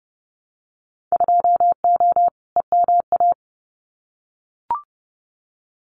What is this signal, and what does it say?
Morse code, heard as a keyed tone, about 30 wpm, 700 Hz. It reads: 2O EMA
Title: Winter Field Day Exchanges 30wpm